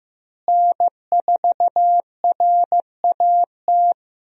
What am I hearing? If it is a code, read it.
Morse code, heard as a keyed tone, about 15 wpm, 700 Hz. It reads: N4RAT
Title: USA Call Signs - 5 Characters - Repeated 2x - 15wpm